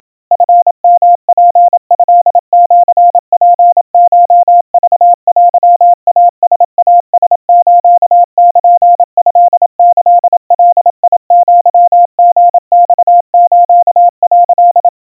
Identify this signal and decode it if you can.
Morse code (Japanese Wabun code), heard as a keyed tone, about 27 wpm, 700 Hz. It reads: チヨツトシツコクテイライラスルトキガアリマス」
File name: kawa / Morse sample / kamyu 100 cpm 700 Hz Crispy